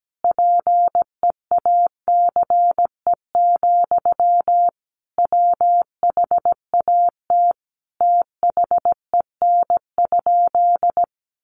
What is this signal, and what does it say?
Morse code, heard as a keyed tone, about 17 wpm, 700 Hz. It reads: PEACE, WHAT THEN?
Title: Sentences from Top 1500 Words 17wpm